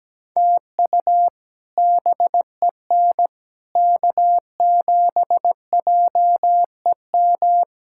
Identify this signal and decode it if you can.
Morse code, heard as a keyed tone, about 17 wpm, 700 Hz. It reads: TU BEN K7JEM
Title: K1USN SST QSO Practice - 17wpm